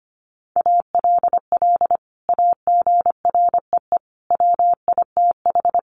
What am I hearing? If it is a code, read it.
Morse code, heard as a keyed tone, about 25 wpm, 690 Hz. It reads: ALL AGREE WITH
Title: Sentences from Top 1100 Words 25wpm